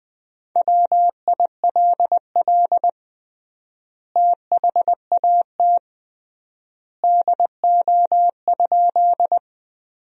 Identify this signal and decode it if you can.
Morse code, heard as a keyed tone, about 21 wpm, 700 Hz. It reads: WILL THAT DO?
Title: 3x Word-Spacing - Sentences from Top 100 Words - 20wpm